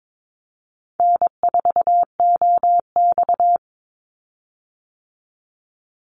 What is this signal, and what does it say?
Morse code, heard as a keyed tone, about 22 wpm, 700 Hz. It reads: N4OX
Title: Easy Call Signs Repeated 3x - 22wpm